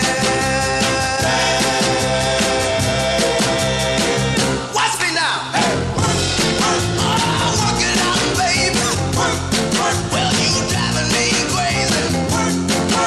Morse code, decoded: none